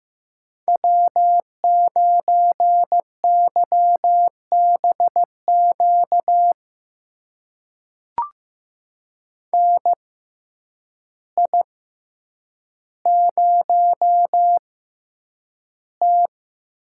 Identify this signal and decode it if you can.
Morse code, heard as a keyed tone, about 15 wpm, spaced as for 8 wpm, 700 Hz. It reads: W9YBQ NI0T
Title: USA Call Signs - 6x Character Spacing - 15wpm